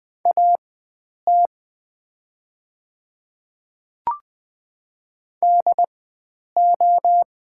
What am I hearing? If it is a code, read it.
Morse code, heard as a keyed tone, about 20 wpm, spaced as for 9 wpm, 700 Hz. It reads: AT DO
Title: 4x Character Spacing - Top 100 Words - 20wpm